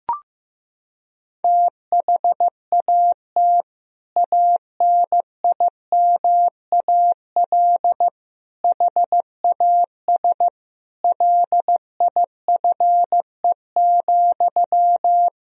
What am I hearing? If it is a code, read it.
Morse code, heard as a keyed tone, about 15 wpm, 700 Hz. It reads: THAT ANIMAL HAS LIFE,
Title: Sentences from Top 1000 Words - Review Part 1 - 15wpm